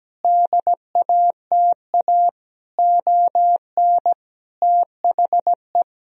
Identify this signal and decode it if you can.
Morse code, heard as a keyed tone, about 17 wpm, 700 Hz. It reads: DATA ON THE
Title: Sentences from Top 1000 Words - Review Part 3 - 17wpm